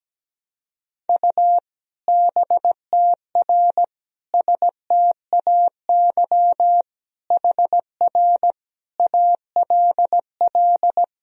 Morse code, heard as a keyed tone, about 17 wpm, 700 Hz. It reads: U BTR STAY HR ALL
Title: Sentences from Top 500 Words and Abbreviations 17wpm